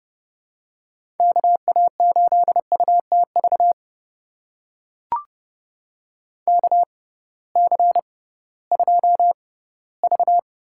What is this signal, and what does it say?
Morse code, heard as a keyed tone, about 31 wpm, spaced as for 18 wpm, 700 Hz. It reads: KA8UTV KC2V
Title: USA Call Signs - 6 Characters - 6x Character Spacing - 30wpm